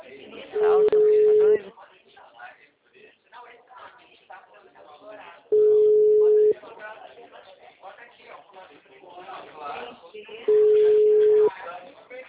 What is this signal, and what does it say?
Ringback tone (Europe, 425 Hz), with speech.
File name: out- -2016- - - .28056